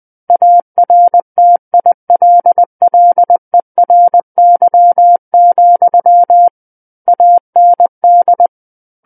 Morse code, heard as a keyed tone, about 20 wpm, 700 Hz. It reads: ARTILLERY, AND
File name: 20 WPM